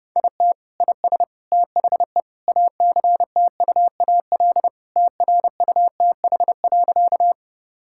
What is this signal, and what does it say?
Morse code, heard as a keyed tone, about 30 wpm, 700 Hz. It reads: IT IS THE ACTUAL TRUTH.
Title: Sentences from Top 1000 Words - Review Part 5 - 30wpm